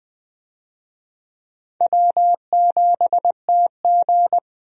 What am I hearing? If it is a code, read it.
Morse code, heard as a keyed tone, about 20 wpm, 700 Hz. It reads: W7TG
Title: USA Call Signs - 4 Characters - Repeated 2x - 20wpm